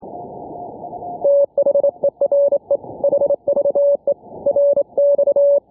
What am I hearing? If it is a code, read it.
Morse code, heard as a keyed tone, about 28 wpm, 570 Hz. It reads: THERE HVE RX
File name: n6tr / HomeBrewTransmitter / N6TR 0120z 7019kHz